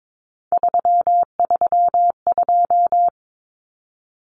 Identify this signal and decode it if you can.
Morse code, heard as a keyed tone, about 22 wpm, 700 Hz. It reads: 332
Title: Common Cut Numbers - 3-Digit 22wpm